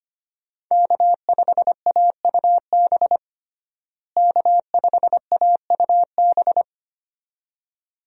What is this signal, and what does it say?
Morse code, heard as a keyed tone, about 25 wpm, 700 Hz. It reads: K5AUB K5AUB
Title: USA Call Signs - 5 Characters - Repeated 2x - 25wpm